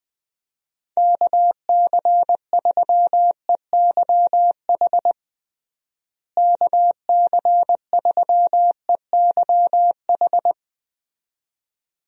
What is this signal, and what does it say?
Morse code, heard as a keyed tone, about 21 wpm, 700 Hz. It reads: KC3EYH KC3EYH
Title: USA Call Signs - 6 Characters - Repeated 2x - 20wpm